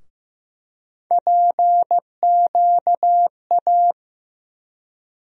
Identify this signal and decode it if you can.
Morse code, heard as a keyed tone, about 15 wpm, 700 Hz. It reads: PQA
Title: USA Call Sign Postfixes - 3 Letters 15wpm